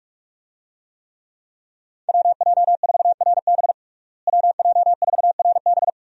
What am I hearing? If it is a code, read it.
Morse code, heard as a keyed tone, about 41 wpm, 700 Hz. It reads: WJ4RB WJ4RB
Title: USA Call Signs - 5 Characters - Repeated 3x - 45wpm